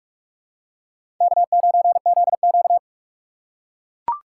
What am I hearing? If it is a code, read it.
Morse code, heard as a keyed tone, about 44 wpm, 690 Hz. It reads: K9ZQ
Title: USA Call Signs - 4 Characters - 3x Character Spacing - 45wpm